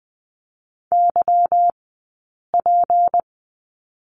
Morse code, heard as a keyed tone, about 20 wpm, spaced as for 8 wpm, 700 Hz. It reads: YP
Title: 151 Lesson 25d - Letter p - Two Characters 20wpm